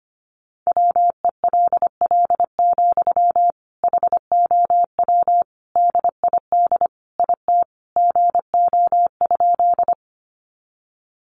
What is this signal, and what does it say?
Morse code, heard as a keyed tone, about 25 wpm, 700 Hz. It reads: WELL, HOW DID IT GO?
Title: Sentences from Top 200 Words 25wpm